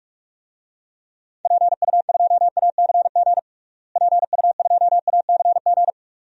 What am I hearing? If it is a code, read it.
Morse code, heard as a keyed tone, about 42 wpm, 700 Hz. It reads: PU2ACG PU2ACG
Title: DXCC-100 Call Signs - 6 Characters - Repeated 2x - 45wpm